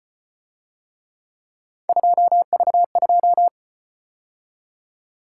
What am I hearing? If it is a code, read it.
Morse code, heard as a keyed tone, about 33 wpm, 700 Hz. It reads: 2V2